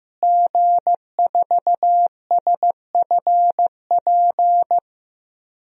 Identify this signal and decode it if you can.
Morse code, heard as a keyed tone, about 15 wpm, 700 Hz. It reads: G4SFP
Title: DXCC-100 Call Signs - 5 Characters - Repeated 2x - 15wpm